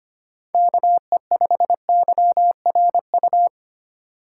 Morse code, heard as a keyed tone, about 25 wpm, 700 Hz. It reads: KE5YRU
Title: USA Call Signs - 6 Characters - 4x Character Spacing - 25wpm